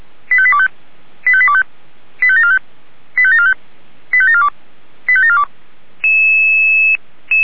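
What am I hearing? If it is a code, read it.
Five-tone selective call (ZVEI), heard as quick runs of five tones, tones of about 70 ms. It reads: 85825 85825 85745 85745 85742 85742
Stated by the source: ZVEI five-tone call 85749 received via SDR by Feuerwehr Orlamünde.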